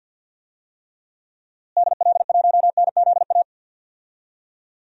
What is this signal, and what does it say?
Morse code, heard as a keyed tone, about 52 wpm, 700 Hz. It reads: DL1NZA
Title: DXCC-100 Call Signs - 6 Characters - Repeated 3x - 50wpm